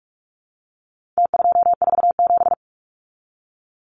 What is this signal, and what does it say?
Morse code, heard as a keyed tone, about 44 wpm, 700 Hz. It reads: T247
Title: Numbers - Contest Style - 4-Digit 45wpm